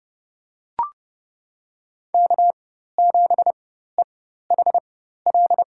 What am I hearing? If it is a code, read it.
Morse code, heard as a keyed tone, about 30 wpm, spaced as for 14 wpm, 700 Hz. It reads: K7EHL
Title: USA Call Signs - 5 Characters - 4x Character Spacing - 30wpm